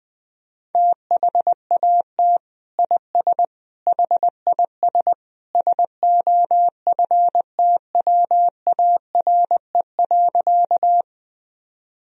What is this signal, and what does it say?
Morse code, heard as a keyed tone, about 19 wpm, 700 Hz. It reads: THAT IS HIS SOFTWARE.
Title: Sentences from Top 1000 Words - Review Part 3 - 20wpm